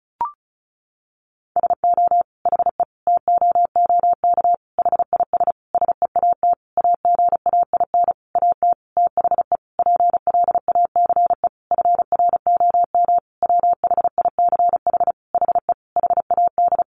Morse code, heard as a keyed tone, about 34 wpm, 700 Hz. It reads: SO HE TOOK HIS SEAT AGAIN AT THE PLACE FROM WHICH HE HAD
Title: Sentences from Top 1500 Words 35wpm